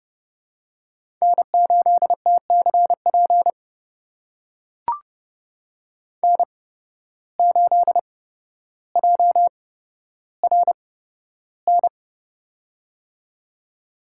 Morse code, heard as a keyed tone, about 30 wpm, spaced as for 13 wpm, 700 Hz. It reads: N8TCP N8JRN
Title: USA Call Signs - 5 Characters - 8x Character Spacing - 30wpm